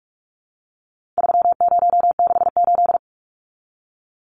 Morse code, heard as a keyed tone, about 44 wpm, 700 Hz. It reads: 3068
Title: Numbers - 4-Digit 45wpm